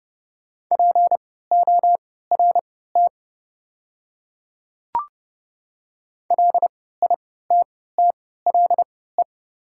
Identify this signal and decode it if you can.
Morse code, heard as a keyed tone, about 30 wpm, spaced as for 17 wpm, 690 Hz. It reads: PORT LITTLE
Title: 3x Character Spacing - Top 500 Words - 30wpm